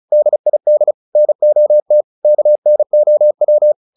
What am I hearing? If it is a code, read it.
Morse code, heard as a keyed tone, about 35 wpm, 600 Hz. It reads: DID NOT KNOW
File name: the old man and the sea 2 35wpm